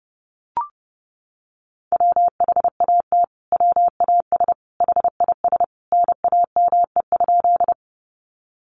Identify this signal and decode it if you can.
Morse code, heard as a keyed tone, about 28 wpm, 700 Hz. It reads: WHAT WAS HIS NAME?
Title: Sentences from Top 200 Words 30wpm